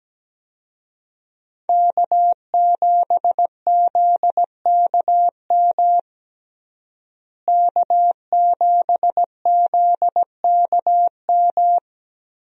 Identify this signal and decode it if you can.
Morse code, heard as a keyed tone, about 17 wpm, 700 Hz. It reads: K7ZKM K7ZKM
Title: USA Call Signs - 5 Characters - Repeated 2x - 17wpm